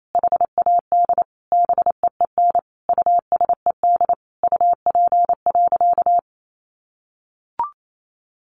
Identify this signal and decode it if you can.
Morse code, heard as a keyed tone, about 28 wpm, 700 Hz. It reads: HAD BEEN USED UP.
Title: Sentences from Top 900 Words 28wpm